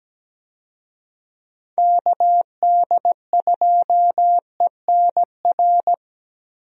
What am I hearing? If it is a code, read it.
Morse code, heard as a keyed tone, about 17 wpm, 700 Hz. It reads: KD2ENR
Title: USA Call Signs - 6 Characters - Repeated 2x - 17wpm